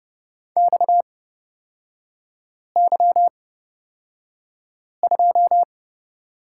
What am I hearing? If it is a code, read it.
Morse code, heard as a keyed tone, about 30 wpm, spaced as for 5 wpm, 700 Hz. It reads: XY2
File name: Lesson 37e - Letter x - Three Characters 30wpm